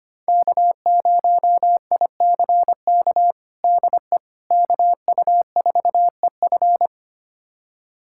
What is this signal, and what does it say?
Morse code, heard as a keyed tone, about 25 wpm, 700 Hz. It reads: K0ICK DE KU4EF
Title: USA Callsign DE Callsign - 5 Characters - 25wpm